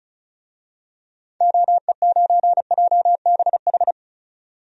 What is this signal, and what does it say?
Morse code, heard as a keyed tone, about 34 wpm, 700 Hz. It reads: OE9JBH
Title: DXCC-100 Call Signs - 6 Characters - 35wpm